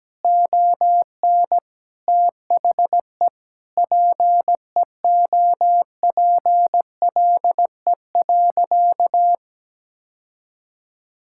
Morse code, heard as a keyed tone, about 17 wpm, 700 Hz. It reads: ON THE PEOPLE.
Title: Sentences from Top 1000 Words - Review Part 5 - 17wpm